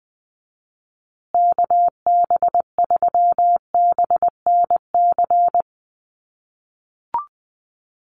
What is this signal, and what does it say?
Morse code, heard as a keyed tone, about 20 wpm, 700 Hz. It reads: KB3BNC
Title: USA Call Signs Repeated 3x 20wpm